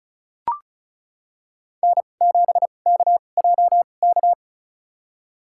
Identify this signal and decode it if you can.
Morse code, heard as a keyed tone, about 36 wpm, spaced as for 25 wpm, 700 Hz. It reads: N7KJK
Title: USA Call Signs - 5 Characters - 2x Character Spacing - 35wpm